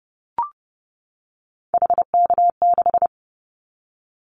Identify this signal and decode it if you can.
Morse code, heard as a keyed tone, about 30 wpm, 700 Hz. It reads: HK6
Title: Callsign Prefixes for Top 100 DXCC - 30wpm